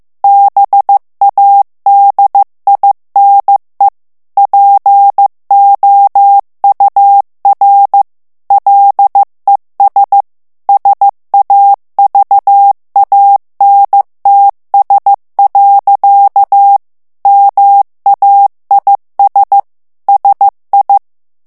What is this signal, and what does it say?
Morse code, heard as a keyed tone, about 15 wpm, 800 Hz. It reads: BADINE POUR LES SAVANTS. MAIS SI
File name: FRANCAIS TEXTE CLAIR 2 VITESSE 14 MOTS MINUTES